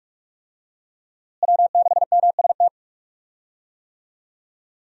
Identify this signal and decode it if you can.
Morse code, heard as a keyed tone, about 41 wpm, 700 Hz. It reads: W6MST
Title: USA Call Signs - 5 Characters - Repeated 3x - 45wpm